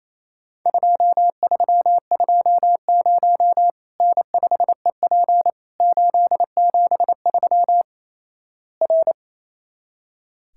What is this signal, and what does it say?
Morse code, heard as a keyed tone, about 28 wpm, 700 Hz. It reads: 2320 N5EP 873 R
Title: Worked All Europe DX Contest - QTC Exchange - 28wpm